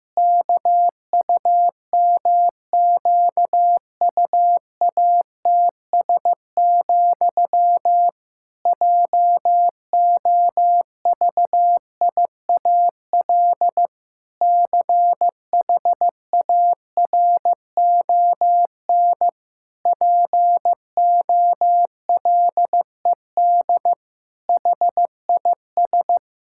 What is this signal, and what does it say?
Morse code, heard as a keyed tone, about 15 wpm, 700 Hz. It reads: KUMQUATS, JOVIAL CHARON POLED HIS
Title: Pangram Challenge v2 - 15wpm